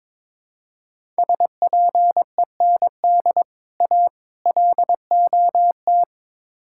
Morse code, heard as a keyed tone, about 22 wpm, 700 Hz. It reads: SPEND A LOT
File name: practice / 22wpm / Build A Phrase - 5-Word Phrases - 22wpm